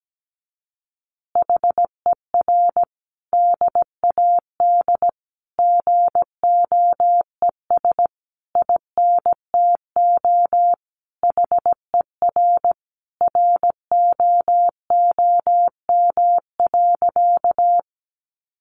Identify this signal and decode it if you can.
Morse code, heard as a keyed tone, about 17 wpm, 700 Hz. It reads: HER DAD GOES INTO HER ROOM.
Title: Additional Easy Phrases 17wpm